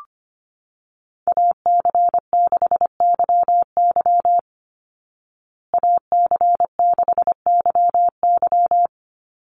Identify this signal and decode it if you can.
Morse code, heard as a keyed tone, about 25 wpm, 700 Hz. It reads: AC6YY AC6YY
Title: Call Signs Repeated 2x - 25wpm